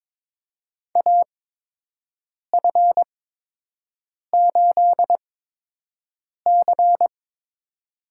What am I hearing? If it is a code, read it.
Morse code, heard as a keyed tone, about 23 wpm, spaced as for 6 wpm, 700 Hz. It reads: AF8C
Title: USA Call Signs - 4 Characters - 8x Character Spacing - 22wpm